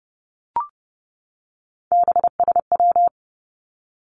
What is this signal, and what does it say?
Morse code, heard as a keyed tone, about 32 wpm, 700 Hz. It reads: BSW
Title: Three Challenging Letter-Number Combinations 30wpm